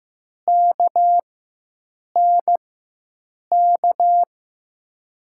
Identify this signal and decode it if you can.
Morse code, heard as a keyed tone, about 15 wpm, spaced as for 7 wpm, 700 Hz. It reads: KNK
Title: USA Call Signs - 6 Characters - 4x Character Spacing - 15wpm